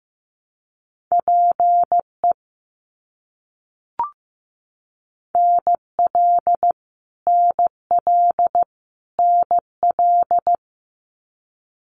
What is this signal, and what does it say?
Morse code, heard as a keyed tone, about 15 wpm, 700 Hz. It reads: PE NL NL NL
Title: Canadian Provinces and Territories - Repeated 3x - 15wpm